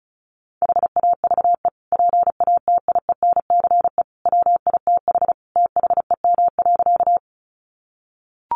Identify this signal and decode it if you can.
Morse code, heard as a keyed tone, about 34 wpm, 700 Hz. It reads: HAVE PATIENCE WITH THEM.
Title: Sentences from Top 1300 Words 35wpm